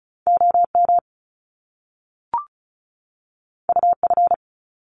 Morse code, heard as a keyed tone, about 36 wpm, 700 Hz. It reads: OM UF
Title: Top Two Letter Combinations in Words 35wpm